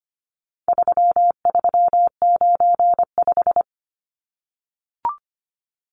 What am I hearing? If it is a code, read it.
Morse code, heard as a keyed tone, about 25 wpm, 700 Hz. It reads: 3395
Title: Numbers - 4-Digit 25wpm